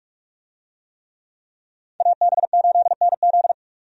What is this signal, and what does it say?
Morse code, heard as a keyed tone, about 43 wpm, 700 Hz. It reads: AB8NZ
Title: USA Call Signs - 5 Characters - 45wpm